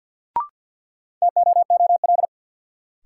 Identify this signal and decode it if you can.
Morse code, heard as a keyed tone, about 48 wpm, 700 Hz. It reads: TOOL
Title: Sentences from Top 1000 Words - Review Part 6 - New Word Drill - 50wpm